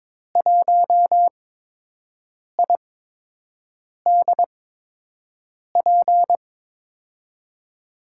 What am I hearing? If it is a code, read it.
Morse code, heard as a keyed tone, about 22 wpm, spaced as for 6 wpm, 700 Hz. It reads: 1IDP
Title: USA Call Signs - 5 Characters - 8x Character Spacing - 22wpm